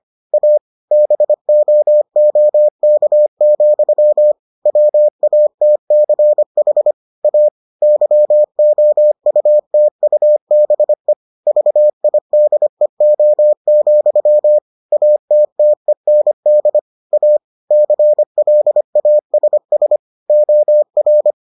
Morse code, heard as a keyed tone, about 25 wpm, 600 Hz. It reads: A BOOK, WATCH A YOUTUBE VIDEO, ATTEND A CLASS OR